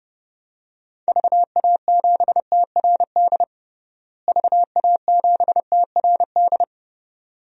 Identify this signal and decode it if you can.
Morse code, heard as a keyed tone, about 32 wpm, 700 Hz. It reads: VA7TRD VA7TRD
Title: DXCC-100 Call Signs - 6 Characters - Repeated 2x - 30wpm